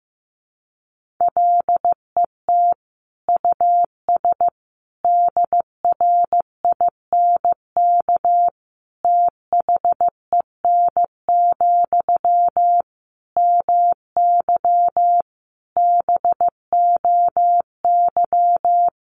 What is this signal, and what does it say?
Morse code, heard as a keyed tone, about 15 wpm, 700 Hz. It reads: LET US DRINK THEN, MY BOY
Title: Sentences from Top 1500 Words 15wpm